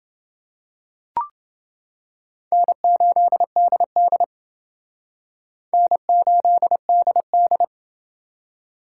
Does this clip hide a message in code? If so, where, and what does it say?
Yes: Morse code from 2.5 to 7.6 s, heard as a keyed tone, about 29 wpm, 700 Hz. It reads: N8DD N8DD